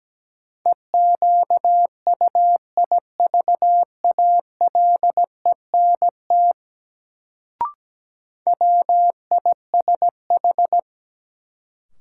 Morse code, heard as a keyed tone, about 17 wpm, 700 Hz. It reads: EQUIVALENT WISH
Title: Sentences from Top 1600 Words - New Word Drill - 17wpm